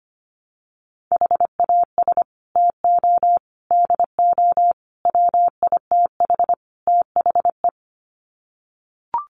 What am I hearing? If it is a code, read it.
Morse code, heard as a keyed tone, about 25 wpm, 700 Hz. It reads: HAS TO DO WITH THE
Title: Sets of 5 Words 25wpm